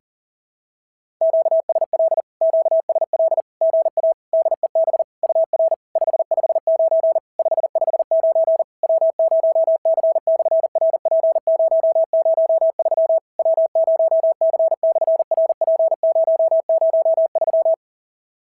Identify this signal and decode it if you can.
Morse code, heard as a keyed tone, about 40 wpm, 690 Hz. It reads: QSL QSL GA DEB UR 559 559 W0C/RP003 W0C/RP003
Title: SOTA QSO Practice 40wpm